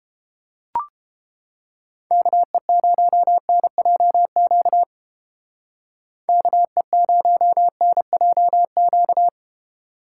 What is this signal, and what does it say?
Morse code, heard as a keyed tone, about 29 wpm, 700 Hz. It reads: KE0NJQ KE0NJQ